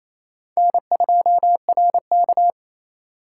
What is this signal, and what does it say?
Morse code, heard as a keyed tone, about 28 wpm, 700 Hz. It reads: N2RK
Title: USA Call Signs - 4 Characters - 4x Character Spacing - 28wpm